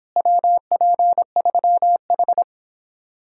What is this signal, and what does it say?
Morse code, heard as a keyed tone, about 29 wpm, 700 Hz. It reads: WP3H